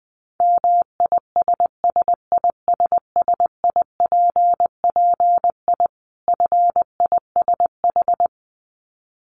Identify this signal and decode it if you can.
Morse code, heard as a keyed tone, about 20 wpm, 700 Hz. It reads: MISSISSIPPI FISH
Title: Dit Challenge 20wpm